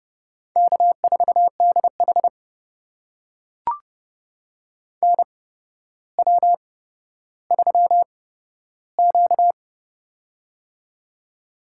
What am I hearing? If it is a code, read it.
Morse code, heard as a keyed tone, about 30 wpm, spaced as for 13 wpm, 700 Hz. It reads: K4DH NW3Q
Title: USA Call Signs - 4 Characters - 8x Character Spacing - 30wpm